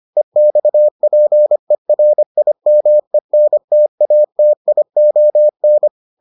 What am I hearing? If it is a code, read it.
Morse code, heard as a keyed tone, about 25 wpm, 600 Hz. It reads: EXPERIMENTATION